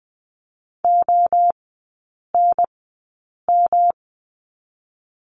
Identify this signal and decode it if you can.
Morse code, heard as a keyed tone, about 20 wpm, spaced as for 8 wpm, 700 Hz. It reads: ONM
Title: 107 Lesson 18e - Letter m - Three Characters 20wpm